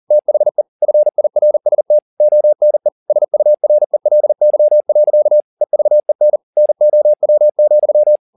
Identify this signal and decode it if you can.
Morse code, heard as a keyed tone, about 40 wpm, 600 Hz. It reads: THE FIRST ONE SURELY. EVEN NOW,
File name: the old man and the sea 2 40wpm